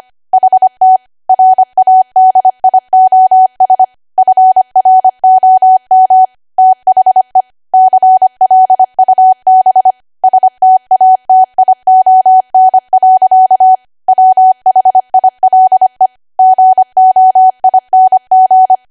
Morse code, heard as a keyed tone, about 25 wpm, 750 Hz. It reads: HT RADIOS FROM THE CLUB STATION. WHILE GOING
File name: Archive / 25 WPM / 25WPM